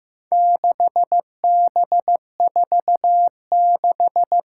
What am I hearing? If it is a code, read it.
Morse code, heard as a keyed tone, about 15 wpm, 700 Hz. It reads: 6B46
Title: Four Challenging Letter-Number Combinations 15wpm